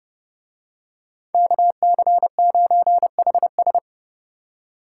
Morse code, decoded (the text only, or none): KC9HS